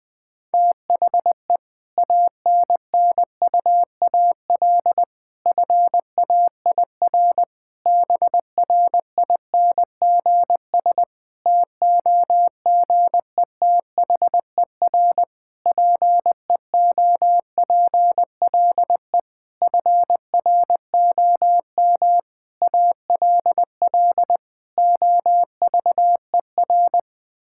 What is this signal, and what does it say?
Morse code, heard as a keyed tone, about 20 wpm, 700 Hz. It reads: THE ANNUAL FAIR BRINGS TOGETHER PEOPLE FROM ALL OVER
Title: Difficult Sentences with Hints - 20wpm